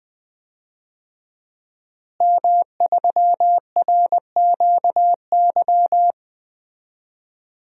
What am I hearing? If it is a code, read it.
Morse code, heard as a keyed tone, about 20 wpm, 700 Hz. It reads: M3RQY
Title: DXCC-100 Call Signs - 5 Characters - Repeated 2x - 20wpm